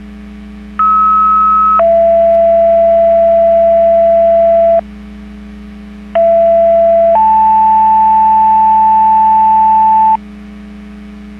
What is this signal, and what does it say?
Two-tone page, heard as a one-second tone, then a three-second tone, about 1290 then 670 Hz, about 670 then 880 Hz. It is heard from 0.8 to 10.2 s.
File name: Portage / 03979 14-08 Portage 154.385 Paging